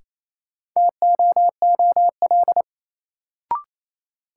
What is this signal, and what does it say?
Morse code, heard as a keyed tone, about 28 wpm, 700 Hz. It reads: TOOL
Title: Sentences from Top 1000 Words - Review Part 6 - New Word Drill - 28wpm